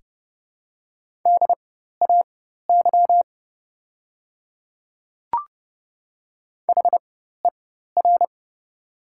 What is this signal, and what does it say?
Morse code, heard as a keyed tone, about 30 wpm, spaced as for 14 wpm, 700 Hz. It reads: DAY HER
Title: 4x Character Spacing - Top 100 Words - 30wpm